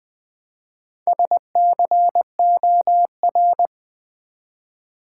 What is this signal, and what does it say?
Morse code, heard as a keyed tone, about 20 wpm, 700 Hz. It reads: SCOR
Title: Build A Word - 1 Letter At A Time - 5-Letter Words - 20wpm